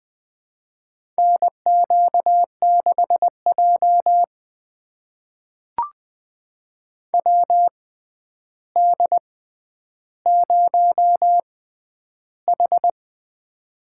Morse code, heard as a keyed tone, about 21 wpm, spaced as for 10 wpm, 700 Hz. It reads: NQ6J WD0H
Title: USA Call Signs - 4 Characters - 6x Character Spacing - 20wpm